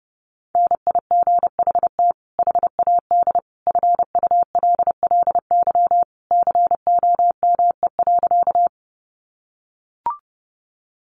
Morse code, heard as a keyed tone, about 30 wpm, 700 Hz. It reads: NIGHT HAD FULLY COME.
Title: Sentences from Top 1300 Words 30wpm